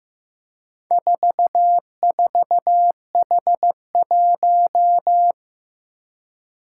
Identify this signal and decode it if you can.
Morse code, heard as a keyed tone, about 15 wpm, 700 Hz. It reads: 44H1
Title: Four Challenging Letter-Number Combinations 15wpm